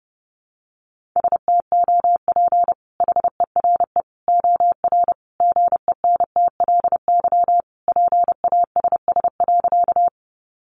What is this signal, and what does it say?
Morse code, heard as a keyed tone, about 30 wpm, 700 Hz. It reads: STOP HERE OR GENTLY PASS.
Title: Sentences from Top 1500 Words 30wpm